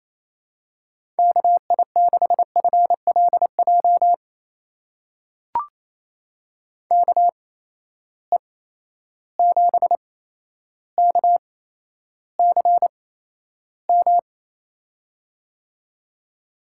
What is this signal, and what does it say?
Morse code, heard as a keyed tone, about 28 wpm, spaced as for 12 wpm, 700 Hz. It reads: KI6FLJ KE7KCM
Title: USA Call Signs - 6 Characters - 8x Character Spacing - 28wpm